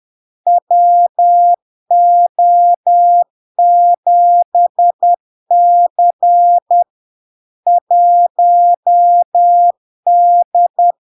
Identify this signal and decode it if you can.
Morse code, heard as a keyed tone, about 10 wpm, 700 Hz. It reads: WO7C 1D
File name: LARK-FD-2022-Log-in-CW-10wpm-700-Hz